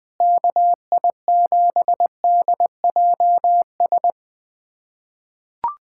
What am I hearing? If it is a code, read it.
Morse code, heard as a keyed tone, about 21 wpm, 700 Hz. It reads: KI7DJS
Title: USA Callsign DE Callsign - 6 Characters - 20wpm